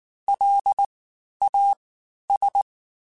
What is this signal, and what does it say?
Morse code, heard as a keyed tone, about 19 wpm, 800 Hz. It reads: L A S
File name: Special INGs-mp3s / Special INGs-mp3s / Fast ing @18